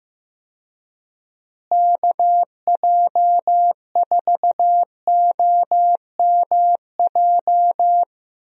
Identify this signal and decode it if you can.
Morse code, heard as a keyed tone, about 15 wpm, 700 Hz. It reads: KJ4OMJ
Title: USA Call Signs - 6 Characters - Repeated 2x - 15wpm